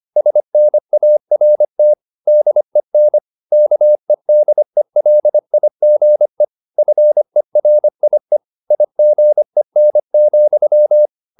Morse code, heard as a keyed tone, about 25 wpm, 600 Hz. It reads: SNART DEN KEDELIGE FERIE IGEN,
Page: Ferie 25 wpm